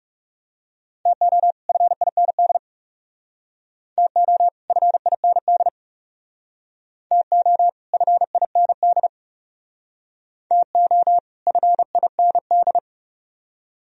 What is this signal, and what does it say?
Morse code, heard as a keyed tone, about 38 wpm, 700 Hz. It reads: TO FIND TO FIND TO FIND TO FIND